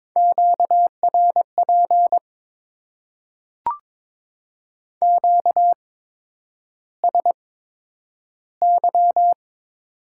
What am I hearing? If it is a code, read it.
Morse code, heard as a keyed tone, about 22 wpm, spaced as for 9 wpm, 700 Hz. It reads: QRP QSY
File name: Top 90 QSO Elements - 8x Character Spacing - 22wpm